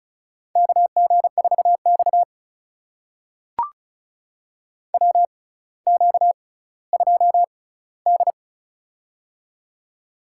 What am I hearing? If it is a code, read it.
Morse code, heard as a keyed tone, about 36 wpm, spaced as for 18 wpm, 700 Hz. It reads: KG4X WQ2D
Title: USA Call Signs - 4 Characters - 6x Character Spacing - 35wpm